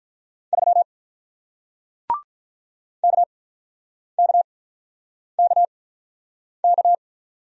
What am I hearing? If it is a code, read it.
Morse code, heard as a keyed tone, about 47 wpm, spaced as for 9 wpm, 700 Hz. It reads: 3 KKKK